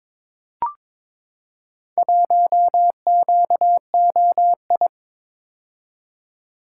Morse code, heard as a keyed tone, about 22 wpm, 700 Hz. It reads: 1QOI
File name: practice / 22wpm / Four Letter-Number Combinations - 22wpm